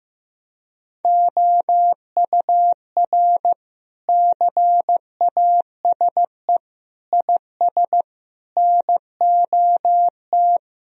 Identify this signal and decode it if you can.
Morse code, heard as a keyed tone, about 15 wpm, 700 Hz. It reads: OUR CASE IS NOT
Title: Sentences from Top 1000 Words - Review Part 2 - 15wpm